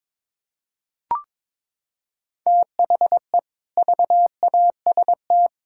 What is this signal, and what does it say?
Morse code, heard as a keyed tone, about 22 wpm, 700 Hz. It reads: THE VAST
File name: practice / 22wpm / Sentences with 4-Word Phrases - 22wpm